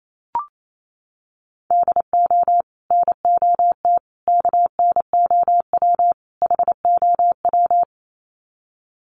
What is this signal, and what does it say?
Morse code, heard as a keyed tone, about 28 wpm, 700 Hz. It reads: DO NOT KNOW HOW
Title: Sets of 4 Words 28wpm